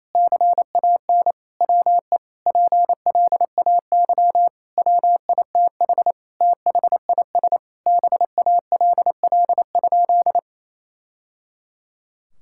Morse code, heard as a keyed tone, about 28 wpm, 700 Hz. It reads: CAN WE PLAY WITH THIS BALL?
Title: Sentences from Top 1000 Words - Review Part 6 - 28wpm